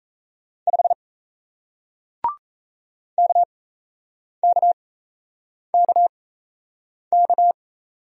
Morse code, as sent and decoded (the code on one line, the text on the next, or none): ..... / -.- -.- -.- -.-
5 KKKK